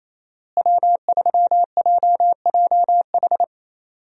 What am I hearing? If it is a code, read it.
Morse code, heard as a keyed tone, about 27 wpm, 700 Hz. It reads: W3JJH
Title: USA Call Signs - 5 Characters - Repeated 2x - 28wpm